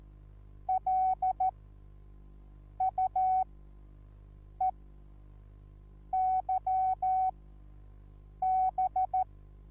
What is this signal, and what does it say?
Morse code, heard as a keyed tone, about 14 wpm, spaced as for 6 wpm, 750 Hz. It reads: LUEYB
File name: Onlinelehrgang / morsen / DJ4UF Morsen21